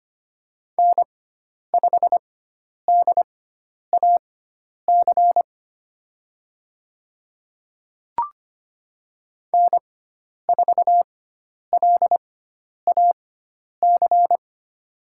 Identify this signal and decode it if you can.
Morse code, heard as a keyed tone, about 25 wpm, spaced as for 10 wpm, 700 Hz. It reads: N5DAC N4LAC
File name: Lesson 16g - Letter c - Call Signs 25wpm